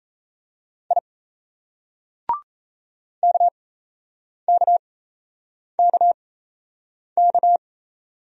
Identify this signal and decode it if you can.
Morse code, heard as a keyed tone, about 33 wpm, spaced as for 8 wpm, 700 Hz. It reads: I KKKK
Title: Speed-Racing Single Letters 28wpm